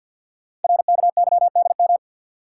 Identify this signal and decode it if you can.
Morse code, heard as a keyed tone, about 51 wpm, 700 Hz. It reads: RKYDM
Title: Code Groups - 5 Letters - 50wpm